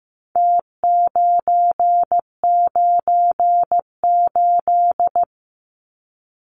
Morse code, heard as a keyed tone, about 15 wpm, 700 Hz. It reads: T998
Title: Numbers - Contest Style - 4-Digit 15wpm